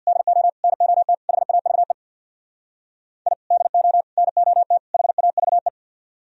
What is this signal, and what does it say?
Morse code, heard as a keyed tone, about 64 wpm, 700 Hz. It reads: DO NOT HAVE I DO NOT HAVE